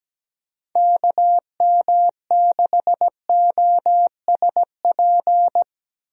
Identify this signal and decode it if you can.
Morse code, heard as a keyed tone, about 17 wpm, 700 Hz. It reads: KM6OSP
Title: USA Call Signs - 6 Characters - Repeated 2x - 17wpm